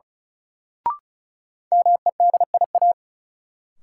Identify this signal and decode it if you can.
Morse code, heard as a keyed tone, about 34 wpm, 700 Hz. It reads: MEDIA
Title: Sentences from Top 1000 Words - Review Part 4 - New Word Drill - 35wpm